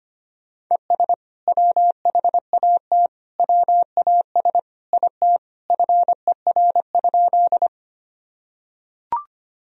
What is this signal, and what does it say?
Morse code, heard as a keyed tone, about 25 wpm, 700 Hz. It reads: ES WHAT WAS IT FER?
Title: Sentences from Top 100 Words and Abbreviations 25wpm